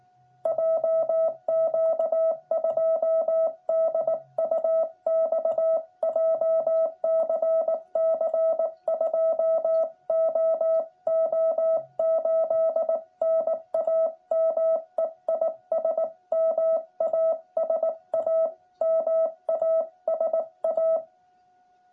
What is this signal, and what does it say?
Morse code, heard as a keyed tone, about 18 wpm, 650 Hz. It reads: JQ2DUX JCC 2OO8 NAME IS MASA MASA
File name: jh8cbh / 2021cw